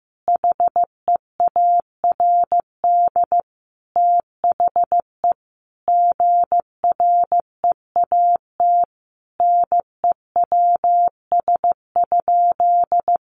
Morse code, heard as a keyed tone, about 15 wpm, 700 Hz. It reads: HEARD THE GREAT NEWS?
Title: Sentences from Top 800 Words 15wpm